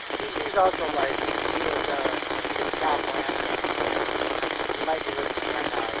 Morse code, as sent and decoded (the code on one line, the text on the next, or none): none